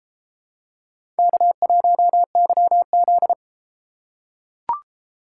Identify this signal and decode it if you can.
Morse code, heard as a keyed tone, about 30 wpm, 700 Hz. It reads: K1YZ